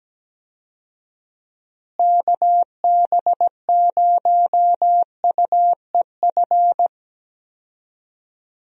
Morse code, heard as a keyed tone, about 17 wpm, 700 Hz. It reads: KB0UEF
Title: USA Call Signs - 6 Characters - Repeated 2x - 17wpm